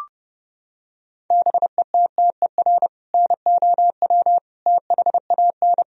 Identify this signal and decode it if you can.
Morse code, heard as a keyed tone, about 30 wpm, 700 Hz. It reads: BETTER NOW THAN
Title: Sentences from Top 1000 Words - Review Part 4 - 30wpm